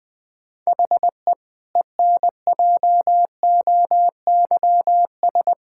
Morse code, heard as a keyed tone, about 21 wpm, 700 Hz. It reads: HE ENJOYS
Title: Difficult Sentences with Hints - 20wpm